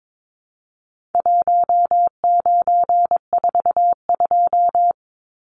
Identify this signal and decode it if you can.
Morse code, heard as a keyed tone, about 22 wpm, 700 Hz. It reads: 1942
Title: Numbers - Contest Style - 4-Digit 22wpm